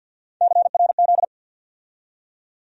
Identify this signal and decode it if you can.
Morse code, heard as a keyed tone, about 49 wpm, 700 Hz. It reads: CRZ
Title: USA Call Sign Postfixes - 3 Letters 50wpm